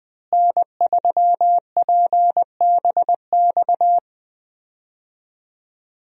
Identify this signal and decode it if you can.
Morse code, heard as a keyed tone, about 21 wpm, 700 Hz. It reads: N3PBX
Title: USA Call Signs - 5 Characters - Repeated 2x - 20wpm